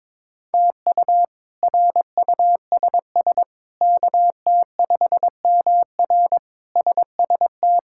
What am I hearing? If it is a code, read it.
Morse code, heard as a keyed tone, about 22 wpm, 700 Hz. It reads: TU RUSS KT5MR SST
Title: K1USN SST QSO Practice - 22wpm